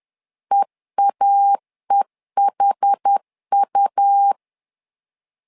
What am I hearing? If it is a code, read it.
Morse code, heard as a keyed tone, about 10 wpm, 800 Hz. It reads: EAEHU
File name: LEZIONI / 2 UHDN / 2 UH 10wpm 1714